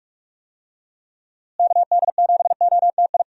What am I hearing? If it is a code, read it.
Morse code, heard as a keyed tone, about 46 wpm, 700 Hz. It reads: KD7OTI
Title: USA Call Signs - 6 Characters - Repeated 2x - 45wpm